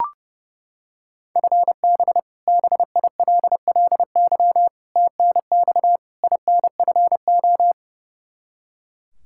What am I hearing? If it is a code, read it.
Morse code, heard as a keyed tone, about 30 wpm, 700 Hz. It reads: FB BILLY TNX INFO
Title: K1USN SST QSO Practice 30wpm